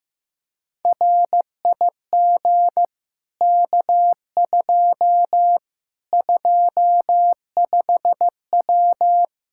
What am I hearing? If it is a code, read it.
Morse code, heard as a keyed tone, about 16 wpm, 700 Hz. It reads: RIG K2 25W
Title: Practice QSO - Exchange 2 - Info - 15wpm